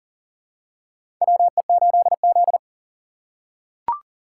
Morse code, heard as a keyed tone, about 40 wpm, 700 Hz. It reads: WE8Z
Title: USA Call Signs - 4 Characters - 6x Character Spacing - 40wpm